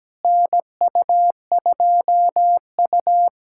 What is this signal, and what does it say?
Morse code, heard as a keyed tone, about 17 wpm, 700 Hz. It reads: NU2U
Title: USA Callsign DE Callsign - 4 Characters - 17wpm